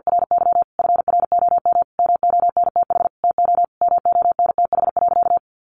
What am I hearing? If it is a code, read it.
Morse code, heard as a keyed tone, about 50 wpm, 700 Hz. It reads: RY FROM MONTH TO MONTH.
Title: Sentences from Top 1200 Words 50wpm